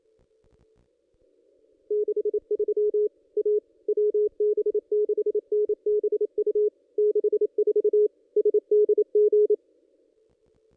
Morse code, heard as a keyed tone, about 28 wpm, 420 Hz. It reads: 63 A WB6NBU 64 SDG
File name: contests / ss-cw-k5tr-op-n5rz -1600